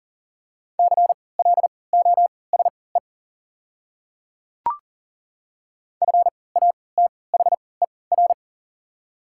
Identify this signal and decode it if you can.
Morse code, heard as a keyed tone, about 40 wpm, spaced as for 23 wpm, 700 Hz. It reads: CLOSE FATHER
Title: 3x Character Spacing - Top 500 Words - 40wpm